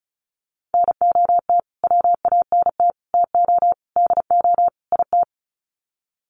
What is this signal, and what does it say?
Morse code, heard as a keyed tone, about 34 wpm, 700 Hz. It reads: NOT WANT TO DO IT
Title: Sets of 5 Words Encore 35wpm